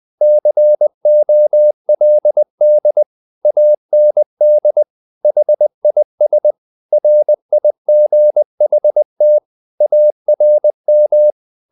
Morse code, heard as a keyed tone, about 20 wpm, 600 Hz. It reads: COLD AND HIS RIGHT ARM
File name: the old man and the sea part 2 20wpm